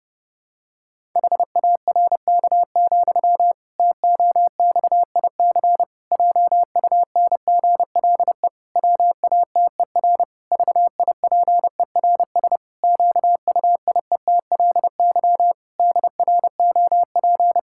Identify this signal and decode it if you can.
Morse code, heard as a keyed tone, about 30 wpm, 700 Hz. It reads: HARK, TOXIC JUNGLE WATER VIPERS QUIETLY DROP
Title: Pangram Challenge v2 - 30wpm